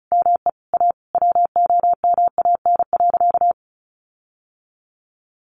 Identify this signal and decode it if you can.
Morse code, heard as a keyed tone, about 34 wpm, 700 Hz. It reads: ME A WOMAN.
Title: Sentences from Top 1500 Words 35wpm